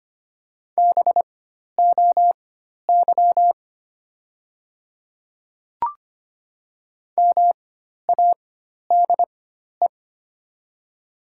Morse code, heard as a keyed tone, about 25 wpm, spaced as for 12 wpm, 700 Hz. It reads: BOY MADE
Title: 4x Character Spacing - Top 200 Words - 25wpm